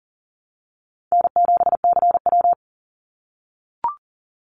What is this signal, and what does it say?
Morse code, heard as a keyed tone, about 40 wpm, 700 Hz. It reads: N7CW
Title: NA SOTA Activator Callsigns 40wpm